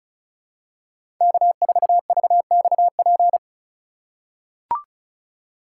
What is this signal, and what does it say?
Morse code, heard as a keyed tone, about 36 wpm, 700 Hz. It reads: K4VXP
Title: USA Call Signs - 2x Character Spacing - 35wpm